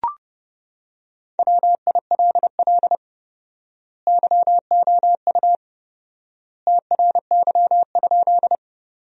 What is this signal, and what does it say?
Morse code, heard as a keyed tone, about 31 wpm, 700 Hz. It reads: WILL YOU TRY?
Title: 4x Word-Spacing - Sentences from Top 200 Words - 30wpm